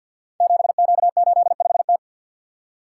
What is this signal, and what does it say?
Morse code, heard as a keyed tone, about 50 wpm, 700 Hz. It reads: 7Q85T
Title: Code Groups - 5 Letter:Numbers - 50wpm